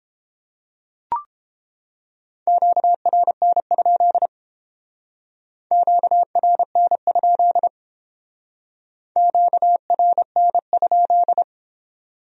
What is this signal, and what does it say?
Morse code, heard as a keyed tone, about 29 wpm, 700 Hz. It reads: QRN? QRN? QRN?